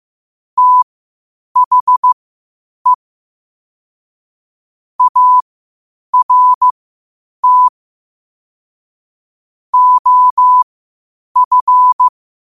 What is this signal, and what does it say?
Morse code, heard as a keyed tone, about 15 wpm, spaced as for 8 wpm, 1000 Hz. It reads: THE ART OF
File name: qotdfiles / 8wpm